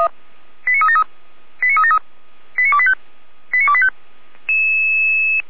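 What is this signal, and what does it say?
Five-tone selective call (ZVEI), heard as quick runs of five tones, tones of about 70 ms. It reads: 89373 89373 89285 89285
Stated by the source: ZVEI five-tone call received on an SDR.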